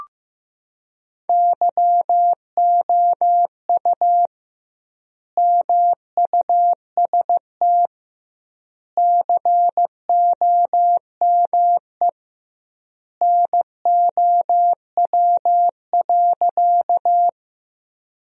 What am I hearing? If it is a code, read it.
Morse code, heard as a keyed tone, about 15 wpm, 700 Hz. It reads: YOU MUST COME NOW.
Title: 2x Word-Spacing - Sentences from Top 200 Words - 15wpm